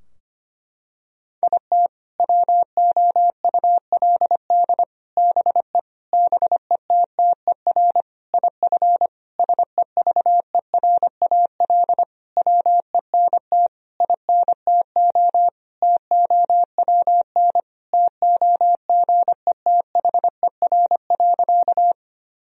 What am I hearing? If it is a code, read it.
Morse code, heard as a keyed tone, about 25 wpm, 700 Hz. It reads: IT WOULD BE BETTER IF SEVERAL WENT INTO TOWN TOGETHER.
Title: Sentences from Top 1000 Words - Review Part 2 - 25wpm